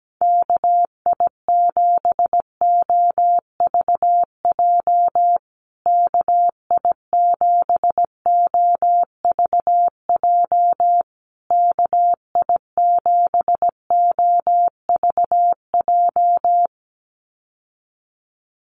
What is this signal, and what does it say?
Morse code, heard as a keyed tone, about 17 wpm, 700 Hz. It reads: KI7OVJ KI7OVJ KI7OVJ
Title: USA Call Signs Repeated 3x 17wpm